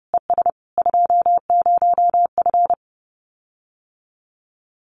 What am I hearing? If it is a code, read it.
Morse code, heard as a keyed tone, about 30 wpm, 700 Hz. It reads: ES 20F
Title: Common QSO Phrases Encore 30wpm